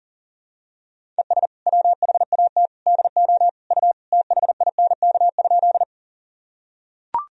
Morse code, heard as a keyed tone, about 40 wpm, 700 Hz. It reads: ES WHAT DO U THINK?
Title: Sentences from Top 200 Words and Abbreviations 40wpm